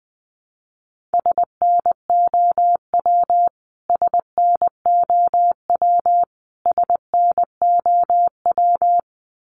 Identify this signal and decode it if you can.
Morse code, heard as a keyed tone, about 20 wpm, 700 Hz. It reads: SNOW SNOW SNOW
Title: Top 500 Words Repeated 20wpm